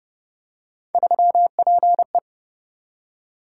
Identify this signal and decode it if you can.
Morse code, heard as a keyed tone, about 27 wpm, 700 Hz. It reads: 3PE